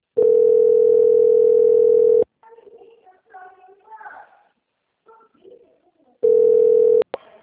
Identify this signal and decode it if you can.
Ringback tone (North America), 440 + 480 Hz.